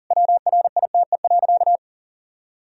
Morse code, heard as a keyed tone, about 40 wpm, 700 Hz. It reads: WRITE.
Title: Sentences from Top 1000 Words - Review Part 1 - 40wpm